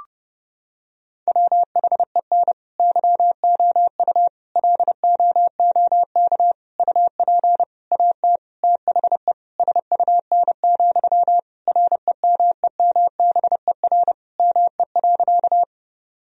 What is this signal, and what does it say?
Morse code, heard as a keyed tone, about 30 wpm, 700 Hz. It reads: WHEN YOU LOOK UP AT THE SUN, REMEMBER ME.
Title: Sentences from Top 1000 Words - Review Part 1 - 30wpm